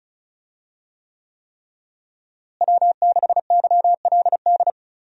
Morse code, heard as a keyed tone, about 36 wpm, 700 Hz. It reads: W6YLD
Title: USA Call Signs - 5 Characters - Repeated 3x - 35wpm